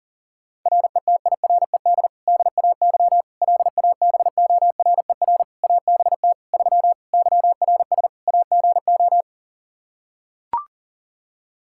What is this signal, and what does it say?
Morse code, heard as a keyed tone, about 38 wpm, 700 Hz. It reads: RETIRED DAY LABORER ABT 3 YRS AGO
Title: Practice QSO - Exchange 3 - More Info - 40wpm